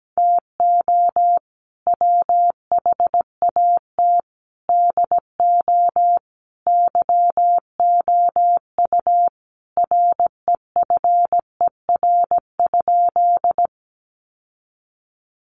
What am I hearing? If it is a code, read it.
Morse code, heard as a keyed tone, about 17 wpm, 700 Hz. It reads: TO WHAT DO YOU REFER?
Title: Sentences from Top 1000 Words 17wpm